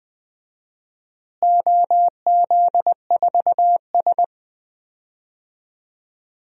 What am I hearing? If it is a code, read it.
Morse code, heard as a keyed tone, about 20 wpm, 700 Hz. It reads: OZ4S
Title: DXCC-100 Call Signs - 4 Characters - Repeated 3x - 20wpm